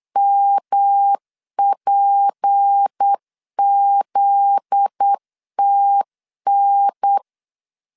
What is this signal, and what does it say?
Morse code, heard as a keyed tone, about 8 wpm, 800 Hz. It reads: MPZTN
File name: LEZIONI / 7 GF9Z / 7 GF9Z 8wpm 3213